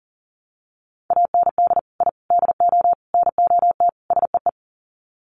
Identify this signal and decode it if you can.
Morse code, heard as a keyed tone, about 40 wpm, 700 Hz. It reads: AND I DO NOT SEE
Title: Sets of 5 Words Encore2 40wpm